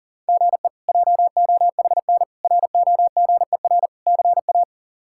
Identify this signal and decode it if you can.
Morse code, heard as a keyed tone, about 40 wpm, 700 Hz. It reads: GE JOHN ROGER CA
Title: K1USN SST QSO Practice - 40wpm